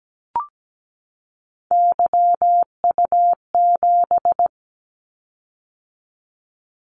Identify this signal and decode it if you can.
Morse code, heard as a keyed tone, about 17 wpm, 700 Hz. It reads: YU7
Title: Callsign Prefixes for Top 25 DXCC - 17wpm